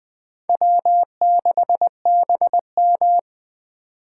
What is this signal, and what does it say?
Morse code, heard as a keyed tone, about 21 wpm, 700 Hz. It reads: W6BM